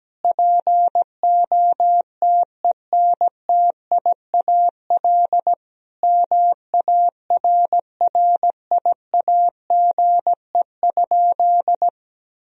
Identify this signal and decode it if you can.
Morse code, heard as a keyed tone, about 17 wpm, 700 Hz. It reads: POTENTIAL MARRIAGE?
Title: Sentences from Top 1000 Words - Review Part 4 - 17wpm